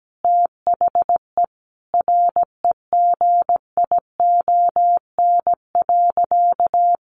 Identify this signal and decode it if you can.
Morse code, heard as a keyed tone, about 17 wpm, 700 Hz. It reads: THE REGION.
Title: Sentences from Top 1500 Words 17wpm